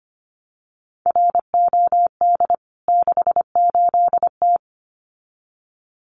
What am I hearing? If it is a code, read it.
Morse code, heard as a keyed tone, about 25 wpm, 700 Hz. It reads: ROD 68T
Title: CWT Exchange 25wpm